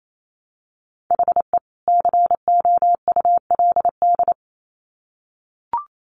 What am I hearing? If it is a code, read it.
Morse code, heard as a keyed tone, about 28 wpm, 700 Hz. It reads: HE COULD
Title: Sets of 2 Words 28wpm